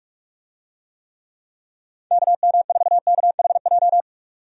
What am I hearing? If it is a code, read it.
Morse code, heard as a keyed tone, about 47 wpm, 700 Hz. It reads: KM4KHJ
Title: USA Call Signs - 6 Characters - 45wpm